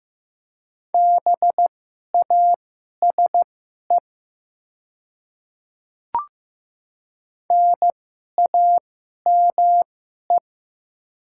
Morse code, heard as a keyed tone, about 15 wpm, spaced as for 11 wpm, 700 Hz. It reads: BASE NAME
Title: 2x Character Spacing - Top 300 Words - 15wpm